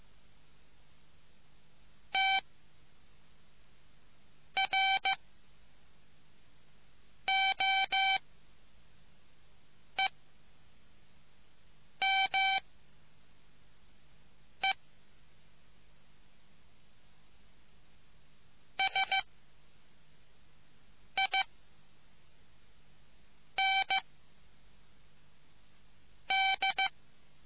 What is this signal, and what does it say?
Morse code, heard as a keyed tone, about 15 wpm, 770 Hz. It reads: T R O E M E S I N D